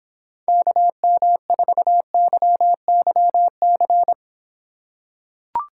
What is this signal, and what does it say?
Morse code, heard as a keyed tone, about 25 wpm, 700 Hz. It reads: KM4YYC